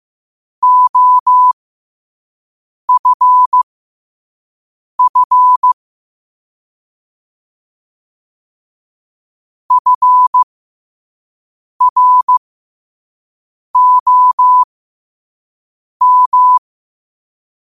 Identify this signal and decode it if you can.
Morse code, heard as a keyed tone, about 15 wpm, spaced as for 5 wpm, 1000 Hz. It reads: OFF FROM
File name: 5wpm